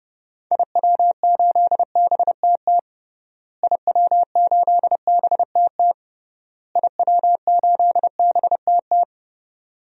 Morse code, heard as a keyed tone, about 32 wpm, 700 Hz. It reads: IW8BTT IW8BTT IW8BTT
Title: DXCC-100 Call Signs - 6 Characters - Repeated 3x - 30wpm